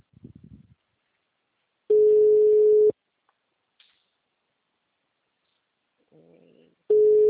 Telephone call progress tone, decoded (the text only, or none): ringback tone